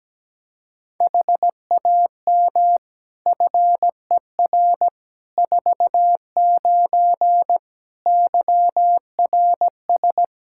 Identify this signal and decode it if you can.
Morse code, heard as a keyed tone, about 17 wpm, 700 Hz. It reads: HAM FER 49 YRS
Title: Practice QSO - Exchange 3 - More Info - 17wpm